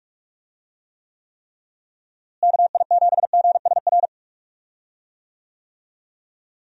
Morse code, heard as a keyed tone, about 43 wpm, 700 Hz. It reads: KI7GSR
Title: USA Call Signs - 6 Characters - Repeated 3x - 45wpm